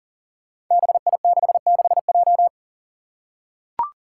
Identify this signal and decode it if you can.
Morse code, heard as a keyed tone, about 40 wpm, 700 Hz. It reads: BI66J
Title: Five Challenging Letter-Number Combinations 40wpm